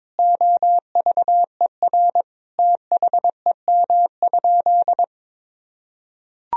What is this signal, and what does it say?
Morse code, heard as a keyed tone, about 22 wpm, 700 Hz. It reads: OVER THEM?
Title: Sentences from Top 1000 Words - Review Part 3 - 22wpm